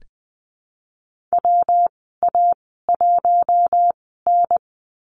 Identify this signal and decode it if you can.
Morse code, heard as a keyed tone, about 20 wpm, spaced as for 15 wpm, 700 Hz. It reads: WA1N
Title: 122 Lesson 20g - Number 3 - Call Signs 20wpm